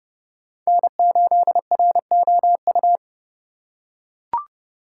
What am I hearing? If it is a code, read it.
Morse code, heard as a keyed tone, about 27 wpm, 700 Hz. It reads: N8ROU